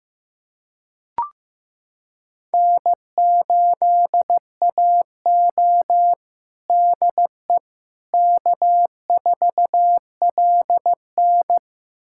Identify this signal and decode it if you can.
Morse code, heard as a keyed tone, about 15 wpm, 700 Hz. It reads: N8AO DE K4LN
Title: USA Callsign DE Callsign - 4 Characters - 15wpm